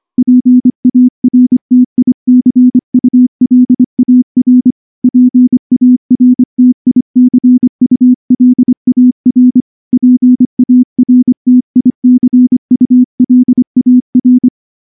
Morse code, heard as a keyed tone, about 24 wpm, 250 Hz. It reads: PARTICULAR PARTICULAR PARTICULAR
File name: morse / morse-training-25WPM